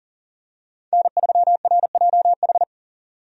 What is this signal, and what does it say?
Morse code, heard as a keyed tone, about 40 wpm, 700 Hz. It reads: N3RJH
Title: USA Call Signs - 5 Characters - 40wpm